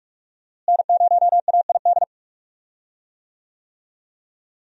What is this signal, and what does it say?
Morse code, heard as a keyed tone, about 47 wpm, 700 Hz. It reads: N0AID
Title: USA Call Signs - 5 Characters - Repeated 2x - 45wpm